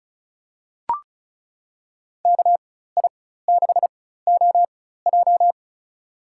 Morse code, heard as a keyed tone, about 36 wpm, spaced as for 16 wpm, 700 Hz. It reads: KI6OJ
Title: USA Call Signs - 5 Characters - 4x Character Spacing - 35wpm